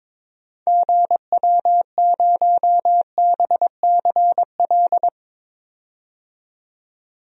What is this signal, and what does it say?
Morse code, heard as a keyed tone, about 22 wpm, 700 Hz. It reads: GW0BCL
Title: DXCC-100 Call Signs - 6 Characters - 22wpm